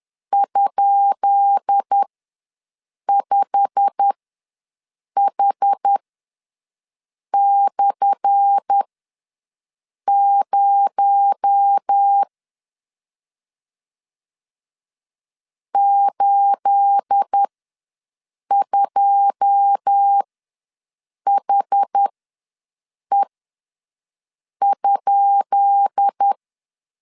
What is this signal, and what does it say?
Morse code, heard as a keyed tone, about 10 wpm, spaced as for 5 wpm, 800 Hz. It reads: ?5H/0 82HE?
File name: LEZIONI / 10 SIGNS / 10B INT ws 10wpm 4134